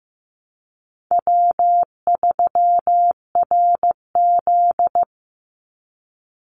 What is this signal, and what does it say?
Morse code, heard as a keyed tone, about 15 wpm, 700 Hz. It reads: W3RZ
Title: Easy Call Signs Repeated 2x - 15wpm